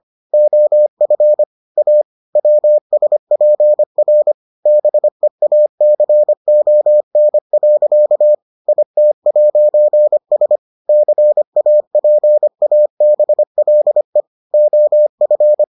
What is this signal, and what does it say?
Morse code, heard as a keyed tone, about 25 wpm, 600 Hz. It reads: OF A WSPR BEACON. IT'S CAPABLE OF